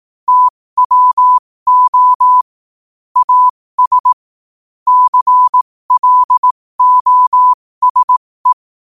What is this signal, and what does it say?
Morse code, heard as a keyed tone, about 18 wpm, spaced as for 15 wpm, 1000 Hz. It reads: TWO AS CLOSE
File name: qotdfiles / 15wpm